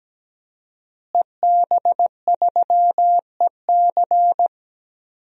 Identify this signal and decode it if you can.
Morse code, heard as a keyed tone, about 17 wpm, 700 Hz. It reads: EB3EC
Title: DXCC-100 Call Signs - 5 Characters - Repeated 2x - 17wpm